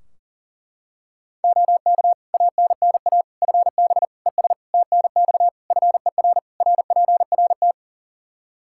Morse code, heard as a keyed tone, about 40 wpm, 700 Hz. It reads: OK ANNA FB ES TNX FER RPRT
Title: Practice QSO - Exchange 2 - Info - 40wpm